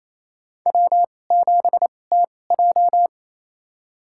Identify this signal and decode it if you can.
Morse code, heard as a keyed tone, about 28 wpm, spaced as for 20 wpm, 700 Hz. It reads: W7TJ
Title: USA Call Signs - 4 Characters - 2x Character Spacing - 28wpm